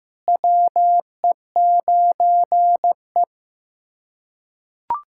Morse code, heard as a keyed tone, about 16 wpm, 700 Hz. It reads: WE9E
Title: SOTA QSO Practice 15wpm